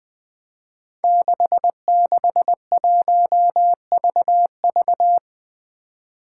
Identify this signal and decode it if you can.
Morse code, heard as a keyed tone, about 20 wpm, 700 Hz. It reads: 661VV
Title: Five Challenging Letter-Number Combinations 20wpm